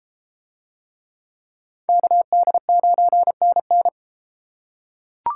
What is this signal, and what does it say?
Morse code, heard as a keyed tone, about 31 wpm, 700 Hz. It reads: KD9NN